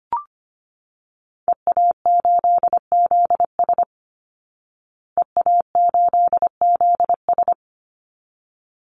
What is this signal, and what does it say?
Morse code, heard as a keyed tone, about 25 wpm, 700 Hz. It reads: EA8ZS EA8ZS
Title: Call Signs Repeated 3x - 25wpm